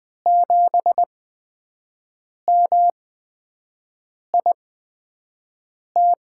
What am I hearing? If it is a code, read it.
Morse code, heard as a keyed tone, about 20 wpm, spaced as for 5 wpm, 700 Hz. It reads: 7MIT
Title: USA Call Signs - 5 Characters - 8x Character Spacing - 20wpm